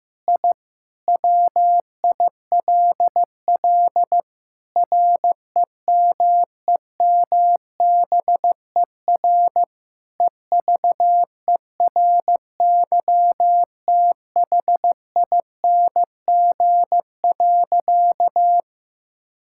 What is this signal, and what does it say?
Morse code, heard as a keyed tone, about 16 wpm, 700 Hz. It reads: I WILL REMEMBER EVERYTHING.
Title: Sentences from Top 1000 Words - Review Part 5 - 15wpm